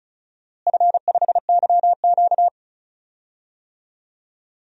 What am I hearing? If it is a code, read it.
Morse code, heard as a keyed tone, about 36 wpm, 700 Hz. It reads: F5YQ
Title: DXCC-100 Call Signs - 4 Characters - Repeated 3x - 35wpm